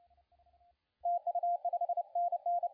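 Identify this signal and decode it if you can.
Morse code, heard as a keyed tone, about 30 wpm, 700 Hz, with radio noise. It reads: TU 5NN
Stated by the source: CW contest audio recorded at LA4C during the SAC CW contest.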